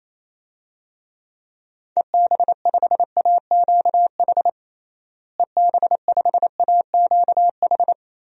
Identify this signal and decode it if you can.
Morse code, heard as a keyed tone, about 28 wpm, 700 Hz. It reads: EB5AQH EB5AQH
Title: DXCC-100 Call Signs - 6 Characters - Repeated 3x - 28wpm